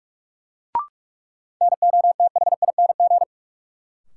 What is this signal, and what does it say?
Morse code, heard as a keyed tone, about 46 wpm, 700 Hz. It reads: NOTHING
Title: Sets of 4 Words - New Word Drill - 45wpm